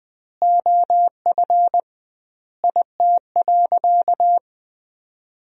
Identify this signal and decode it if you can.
Morse code, heard as a keyed tone, about 20 wpm, 700 Hz. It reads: OF IT.
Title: 2x Word-Spacing - Sentences from Top 400 Words - 20wpm